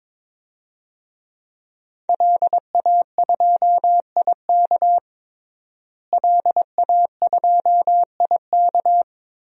Morse code, heard as a keyed tone, about 21 wpm, 700 Hz. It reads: LA2IK LA2IK
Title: DXCC-100 Call Signs - 5 Characters - Repeated 2x - 22wpm